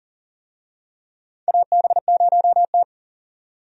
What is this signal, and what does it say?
Morse code, heard as a keyed tone, about 40 wpm, 700 Hz. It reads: AB0T
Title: USA Call Signs - 4 Characters - 40wpm